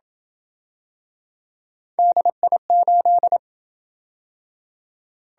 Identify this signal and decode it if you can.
Morse code, heard as a keyed tone, about 27 wpm, 700 Hz. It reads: DI8